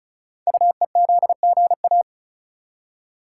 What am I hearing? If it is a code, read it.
Morse code, heard as a keyed tone, about 36 wpm, 690 Hz. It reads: UEZGA
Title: Code Groups - 5 Letter:Numbers - 35wpm